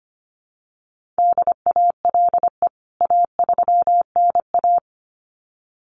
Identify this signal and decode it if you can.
Morse code, heard as a keyed tone, about 25 wpm, 700 Hz. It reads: DALE A3NA
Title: CWT Exchange 25wpm